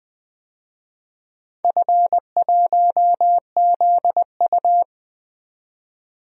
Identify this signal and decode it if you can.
Morse code, heard as a keyed tone, about 20 wpm, 700 Hz. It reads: F1ZU
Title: DXCC-100 Call Signs - 4 Characters - 20wpm